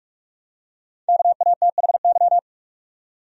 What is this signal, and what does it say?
Morse code, heard as a keyed tone, about 46 wpm, 700 Hz. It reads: KATHY
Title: USA Common Names - 45wpm